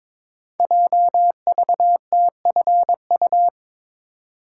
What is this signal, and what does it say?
Morse code, heard as a keyed tone, about 22 wpm, 700 Hz. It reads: JVTFU
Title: Code Groups - 5 Letter:Numbers - 22wpm